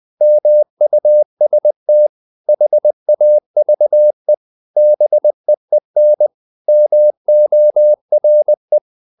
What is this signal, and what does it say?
Morse code, heard as a keyed tone, about 20 wpm, 600 Hz. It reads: MUST HAVE BEEN MORE
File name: the old man and the sea / the old man and the sea part 1 20wpm